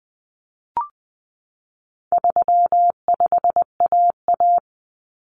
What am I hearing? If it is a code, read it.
Morse code, heard as a keyed tone, about 20 wpm, 700 Hz. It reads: 35AA
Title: Common Cut Numbers - 4-Digit 20wpm